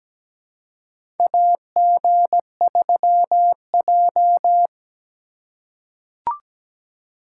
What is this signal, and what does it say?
Morse code, heard as a keyed tone, about 17 wpm, 700 Hz. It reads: AG3J
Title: USA Call Signs - 4 Characters - 2x Character Spacing - 17wpm